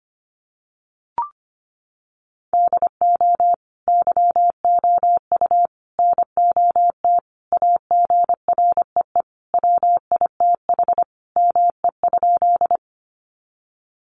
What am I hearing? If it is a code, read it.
Morse code, heard as a keyed tone, about 25 wpm, 690 Hz. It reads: DO YOU NOT AGREE WITH ME?
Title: Sentences from Top 1100 Words 25wpm